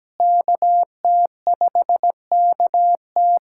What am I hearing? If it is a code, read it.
Morse code, heard as a keyed tone, about 17 wpm, 700 Hz. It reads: KT5KT
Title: USA Callsign DE Callsign - 5 Characters - 17wpm